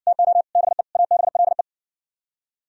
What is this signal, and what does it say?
Morse code, heard as a keyed tone, about 60 wpm, 700 Hz. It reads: TO BE ABLE